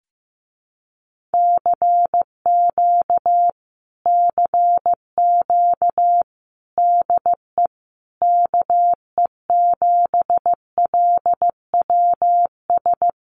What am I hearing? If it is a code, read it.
Morse code, heard as a keyed tone, about 14 wpm, 700 Hz. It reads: CQ CQ DE KE7LWS
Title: Common QSO Phrases Encore 15wpm